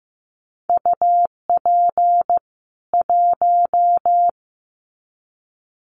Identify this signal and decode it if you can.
Morse code, heard as a keyed tone, about 15 wpm, 700 Hz. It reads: UP 1
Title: Common QSO Phrases 15wpm